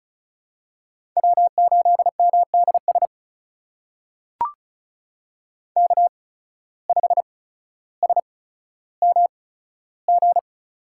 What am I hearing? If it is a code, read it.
Morse code, heard as a keyed tone, about 34 wpm, spaced as for 15 wpm, 700 Hz. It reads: W8MDS K5SMG
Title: USA Call Signs - 5 Characters - 8x Character Spacing - 35wpm